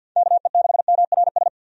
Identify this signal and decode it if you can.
Morse code, heard as a keyed tone, about 52 wpm, 700 Hz. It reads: KE6MRS
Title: USA Call Signs - 6 Characters - 50wpm